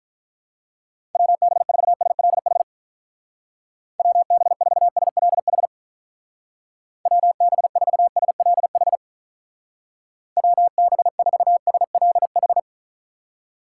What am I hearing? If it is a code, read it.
Morse code, heard as a keyed tone, about 45 wpm, 700 Hz. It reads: WB4SLH WB4SLH WB4SLH WB4SLH